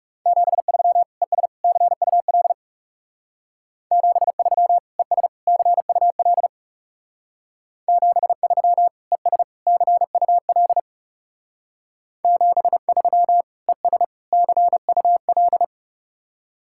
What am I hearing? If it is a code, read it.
Morse code, heard as a keyed tone, about 34 wpm, 700 Hz. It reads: 73 ES CUL 73 ES CUL 73 ES CUL 73 ES CUL
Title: Speed-Racing Common QSO Phrases - 30wpm